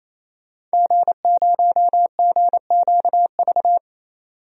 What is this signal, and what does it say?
Morse code, heard as a keyed tone, about 28 wpm, 700 Hz. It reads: G0GQV
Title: DXCC-100 Call Signs - 5 Characters - 28wpm